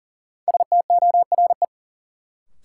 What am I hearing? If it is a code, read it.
Morse code, heard as a keyed tone, about 40 wpm, 700 Hz. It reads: STORE
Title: Sentences from Top 1000 Words - Review Part 5 - New Word Drill - 40wpm